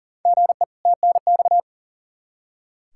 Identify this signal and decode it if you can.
Morse code, heard as a keyed tone, about 40 wpm, 700 Hz. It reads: GE TNX
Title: Practice QSO - Exchange 1 - Beginning - 40wpm